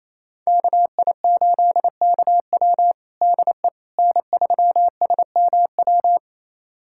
Morse code, heard as a keyed tone, about 28 wpm, 700 Hz. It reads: KI8KW DE N3SMW
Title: USA Callsign DE Callsign - 5 Characters - 28wpm